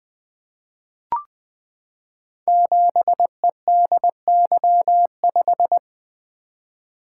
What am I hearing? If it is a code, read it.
Morse code, heard as a keyed tone, about 21 wpm, 700 Hz. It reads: 7EDY5
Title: Code Groups - 5 Letter:Numbers - 20wpm